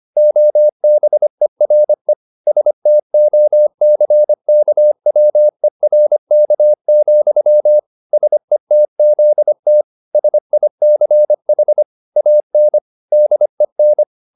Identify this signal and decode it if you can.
Morse code, heard as a keyed tone, about 25 wpm, 600 Hz. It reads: OBERE STOCKWERK, SETZT SICH AN DEN